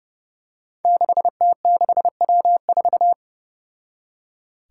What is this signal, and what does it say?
Morse code, heard as a keyed tone, about 32 wpm, 700 Hz. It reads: 6T6W4
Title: Five Challenging Letter-Number Combinations 30wpm